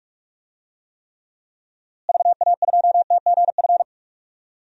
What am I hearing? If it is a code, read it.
Morse code, heard as a keyed tone, about 46 wpm, 700 Hz. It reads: VA2TGF
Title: DXCC-100 Call Signs - 6 Characters - Repeated 3x - 45wpm